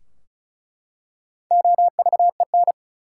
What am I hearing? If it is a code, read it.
Morse code, heard as a keyed tone, about 34 wpm, 700 Hz. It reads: OVEN
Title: Four Letter Words - 35wpm